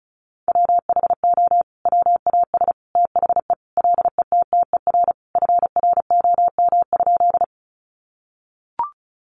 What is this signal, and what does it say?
Morse code, heard as a keyed tone, about 34 wpm, 700 Hz. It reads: WHO WAS THE LETTER FROM?
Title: Sentences from Top 300 Words 35wpm